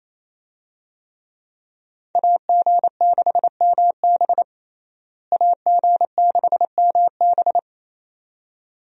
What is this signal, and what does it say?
Morse code, heard as a keyed tone, about 28 wpm, 700 Hz. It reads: AG6MB AG6MB
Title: USA Call Signs - 5 Characters - Repeated 2x - 28wpm